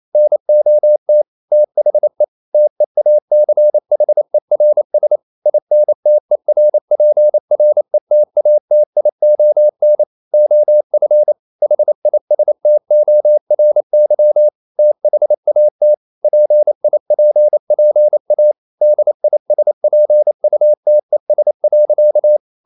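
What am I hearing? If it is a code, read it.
Morse code, heard as a keyed tone, about 28 wpm, 600 Hz. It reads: NOT THE TEACHERS INTERPRETATION OF HISTORY THAT PIPPA DISPUTES.
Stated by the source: Morse code practice at 28 wpm, text of Ken Follett's novel Never in English, from CW med Gnister.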